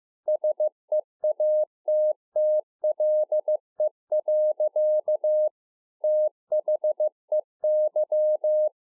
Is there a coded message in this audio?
Yes: Morse code from 0.3 to 8.7 s, heard as a keyed tone, about 15 wpm, 610 Hz. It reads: SEATTLE. THEY